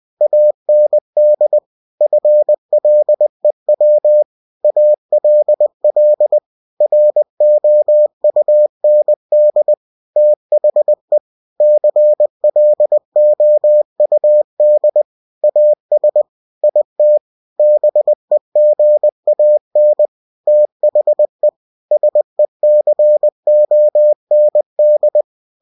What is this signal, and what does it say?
Morse code, heard as a keyed tone, about 20 wpm, 600 Hz. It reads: AND FLEW ALL ROUND THE CLOUD AS IT BEGAN THE SECOND
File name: cw-books / winnie-the-pooh / winnie-20wpm-page-0009